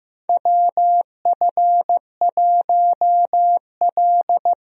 Morse code, heard as a keyed tone, about 15 wpm, 700 Hz. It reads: WF1L
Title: USA Call Signs - 4 Characters - Repeated 3x - 15wpm